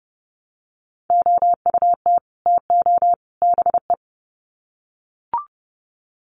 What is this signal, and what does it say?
Morse code, heard as a keyed tone, about 30 wpm, 700 Hz. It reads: OUT TO BE
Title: Sets of 3 Words 30wpm